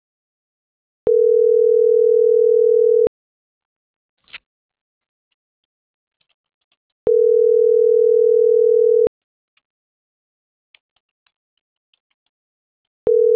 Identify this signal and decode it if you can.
Ringback tone (North America), 440 + 480 Hz.